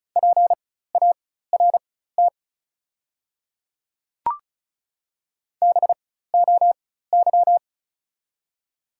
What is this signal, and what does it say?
Morse code, heard as a keyed tone, about 36 wpm, spaced as for 16 wpm, 700 Hz. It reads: PART BOY
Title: 4x Character Spacing - Top 500 Words - 35wpm